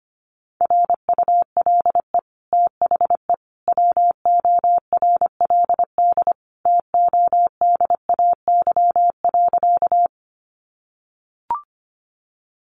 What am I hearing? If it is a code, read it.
Morse code, heard as a keyed tone, about 25 wpm, 700 Hz. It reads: RULE THE WORLD TODAY.
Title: Sentences from Top 600 Words 25wpm